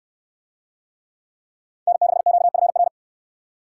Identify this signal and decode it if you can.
Morse code, heard as a keyed tone, about 65 wpm, 700 Hz. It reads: N6QFR